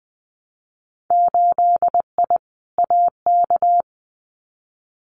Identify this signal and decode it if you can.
Morse code, heard as a keyed tone, about 20 wpm, 700 Hz. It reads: 8I AK
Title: Winter Field Day Exchanges Encore 20wpm